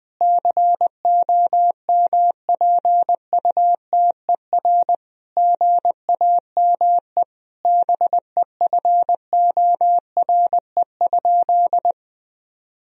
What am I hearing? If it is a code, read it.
Morse code, heard as a keyed tone, about 20 wpm, 700 Hz. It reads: COMPUTER GAME BEFORE?
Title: Sentences from Top 1000 Words - Review Part 3 - 20wpm